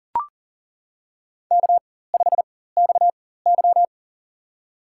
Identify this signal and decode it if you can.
Morse code, heard as a keyed tone, about 40 wpm, spaced as for 19 wpm, 700 Hz. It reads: K5XY
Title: USA Call Signs - 4 Characters - 4x Character Spacing - 40wpm